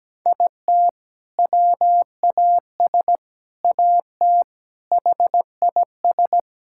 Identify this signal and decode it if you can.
Morse code, heard as a keyed tone, about 17 wpm, 700 Hz. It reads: IT WAS AT HIS
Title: Sentences from Top 1600 Words 17wpm